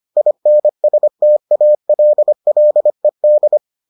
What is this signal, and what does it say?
Morse code, heard as a keyed tone, about 25 wpm, 600 Hz. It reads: INSTALLED